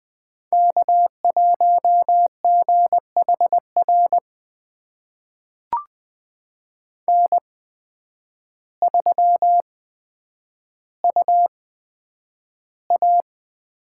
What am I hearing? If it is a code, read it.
Morse code, heard as a keyed tone, about 20 wpm, spaced as for 9 wpm, 700 Hz. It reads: K1GHR N3UA
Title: USA Call Signs - 5 Characters - 8x Character Spacing - 20wpm